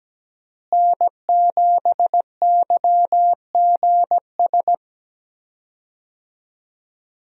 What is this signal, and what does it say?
Morse code, heard as a keyed tone, about 17 wpm, 700 Hz. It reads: N7YGS
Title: USA Call Signs - 5 Characters - Repeated 2x - 17wpm